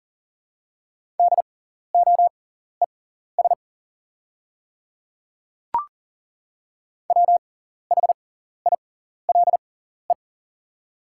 Morse code, heard as a keyed tone, about 40 wpm, spaced as for 14 wpm, 700 Hz. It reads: DOES WHILE
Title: 6x Character Spacing - Top 500 Words - 40wpm